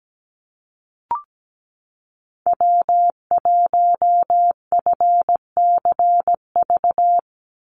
Morse code, heard as a keyed tone, about 17 wpm, 700 Hz. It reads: W1FCV
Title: Call Signs - 17wpm